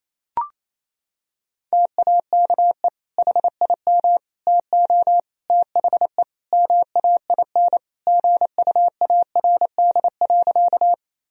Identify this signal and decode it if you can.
Morse code, heard as a keyed tone, about 28 wpm, 700 Hz. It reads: TAKE HIM TO THE MAIN GUARD.
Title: Sentences from Top 1000 Words - Review Part 4 - 28wpm